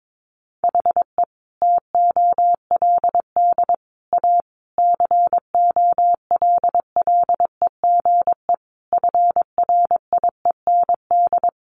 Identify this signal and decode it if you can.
Morse code, heard as a keyed tone, about 22 wpm, 700 Hz. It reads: HE TOLD A COLLEGE FRIEND
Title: Sentences from Top 1300 Words 22wpm